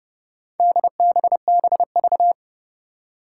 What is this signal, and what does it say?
Morse code, heard as a keyed tone, about 31 wpm, 700 Hz. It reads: DBBV
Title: Four Challenging Letter-Number Combinations 30wpm